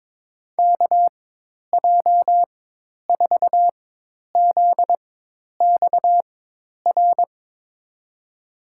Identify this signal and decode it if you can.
Morse code, heard as a keyed tone, about 21 wpm, spaced as for 10 wpm, 700 Hz. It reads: KJ4ZXR
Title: USA Call Signs - 4x Character Spacing - 22wpm